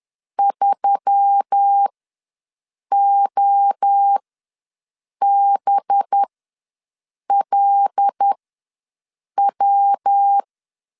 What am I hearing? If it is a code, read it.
Morse code, heard as a keyed tone, about 10 wpm, spaced as for 6 wpm, 800 Hz. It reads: 3OBLW
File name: LEZIONI / 5 LBVC / 5 LB ws 10wpm 3616